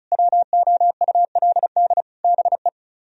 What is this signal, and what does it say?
Morse code, heard as a keyed tone, about 42 wpm, 700 Hz. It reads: WOULD BE
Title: Speed-Racing - Sets of 5 Words 35wpm